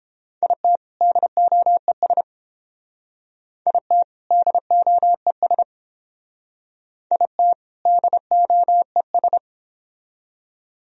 Morse code, heard as a keyed tone, about 30 wpm, 700 Hz. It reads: IT DOES IT DOES IT DOES